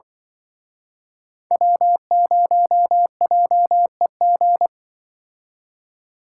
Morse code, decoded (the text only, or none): W0JEG